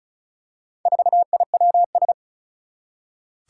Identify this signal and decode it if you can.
Morse code, heard as a keyed tone, about 36 wpm, 700 Hz. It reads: 4IWS
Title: Four Challenging Letter-Number Combinations 35wpm